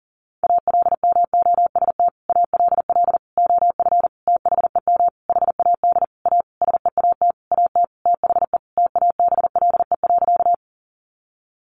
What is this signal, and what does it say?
Morse code, heard as a keyed tone, about 40 wpm, 700 Hz. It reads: ALMOST ALL OF THEM HAD A SEAT AT THE TABLE.
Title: Sentences from Top 1500 Words 40wpm